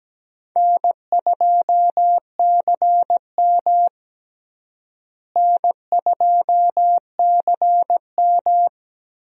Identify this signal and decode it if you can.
Morse code, heard as a keyed tone, about 17 wpm, 700 Hz. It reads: N2CM N2CM
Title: USA Call Signs - 4 Characters - Repeated 3x - 17wpm